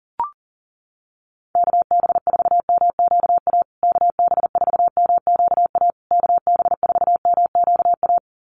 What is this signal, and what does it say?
Morse code, heard as a keyed tone, about 37 wpm, 700 Hz. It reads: KB4MQA KB4MQA KB4MQA
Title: USA Call Signs Repeated 3x 40wpm